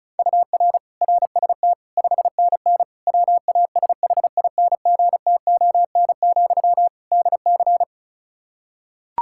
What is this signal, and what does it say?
Morse code, heard as a keyed tone, about 36 wpm, 700 Hz. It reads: UR RST 5NN WASHINGTON, DC
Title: Practice QSO - Exchange 1 - Beginning - 35wpm